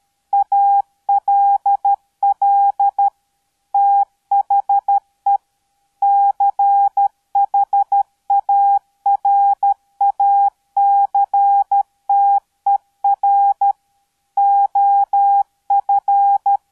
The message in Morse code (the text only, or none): ALL THE CHARACTER OF